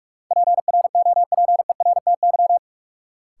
Morse code, heard as a keyed tone, about 46 wpm, 700 Hz. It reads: PROPERTY
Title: Sentences from Top 1000 Words - New Word Drill - 45wpm